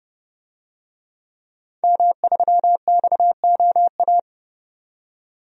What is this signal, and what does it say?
Morse code, heard as a keyed tone, about 30 wpm, 700 Hz. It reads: M3XOA
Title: DXCC-100 Call Signs - 5 Characters - 30wpm